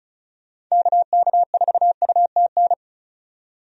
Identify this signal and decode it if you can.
Morse code, heard as a keyed tone, about 34 wpm, 700 Hz. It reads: KK4UTN
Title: USA Call Signs - 6 Characters - 35wpm